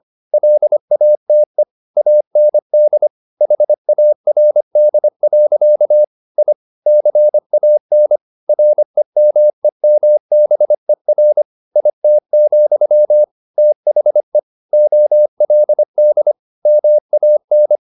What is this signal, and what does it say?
Morse code, heard as a keyed tone, about 25 wpm, 600 Hz. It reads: LATE AND HARD. I CAN REMEMBER IT, THE OLD MAN
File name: the old man and the sea part 1 25wpm